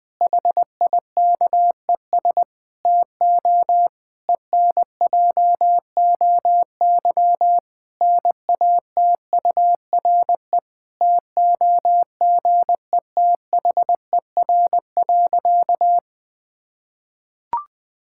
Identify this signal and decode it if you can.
Morse code, heard as a keyed tone, about 21 wpm, 700 Hz. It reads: HIKES TO ENJOY NATURE TOGETHER.
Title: Difficult Sentences with Hints - 20wpm